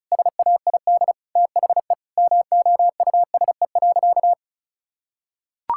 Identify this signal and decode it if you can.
Morse code, heard as a keyed tone, about 36 wpm, 700 Hz. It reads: SAID THE MOUSE.
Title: Sentences from Top 1000 Words - Review Part 5 - 35wpm